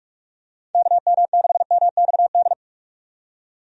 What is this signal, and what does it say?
Morse code, heard as a keyed tone, about 46 wpm, 700 Hz. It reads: KM6MXD
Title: USA Call Signs - 6 Characters - 45wpm